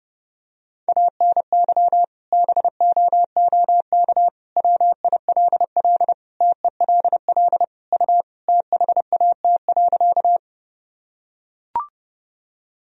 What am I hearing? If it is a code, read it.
Morse code, heard as a keyed tone, about 32 wpm, 700 Hz. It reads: ANY BOOK WILL TELL U THAT.
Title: Sentences from Top 300 Words and Abbreviations 30wpm